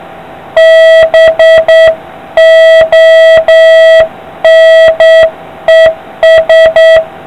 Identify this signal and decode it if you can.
Morse code, heard as a keyed tone, about 8 wpm, 650 Hz, with noise. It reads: BONES